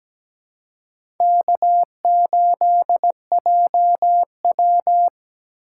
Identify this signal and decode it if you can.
Morse code, heard as a keyed tone, about 17 wpm, 700 Hz. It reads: K8JW
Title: USA Call Signs - 3x Character Spacing - 17wpm